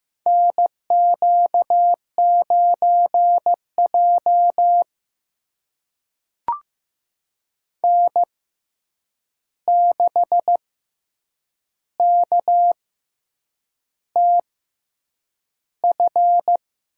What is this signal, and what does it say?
Morse code, heard as a keyed tone, about 15 wpm, spaced as for 7 wpm, 700 Hz. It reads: NQ9J N6KTF
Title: USA Call Signs - 6x Character Spacing - 15wpm